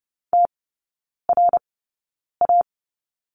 Morse code, heard as a keyed tone, about 30 wpm, spaced as for 9 wpm, 700 Hz. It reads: TRA
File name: Lesson 17e - Letter u - Three Characters 30wpm